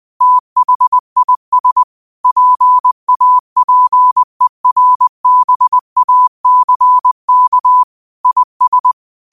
Morse code, heard as a keyed tone, about 20 wpm, 1000 Hz. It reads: THIS PAPERBACK IS
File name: 20wpm